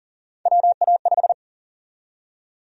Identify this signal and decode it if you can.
Morse code, heard as a keyed tone, about 40 wpm, 700 Hz. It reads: WA5
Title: USA Call Sign Prefixes - 40wpm